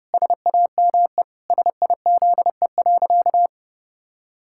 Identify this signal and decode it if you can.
Morse code, heard as a keyed tone, about 32 wpm, 700 Hz. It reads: SAME SIZE.
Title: Pangram Challenge v2 - 30wpm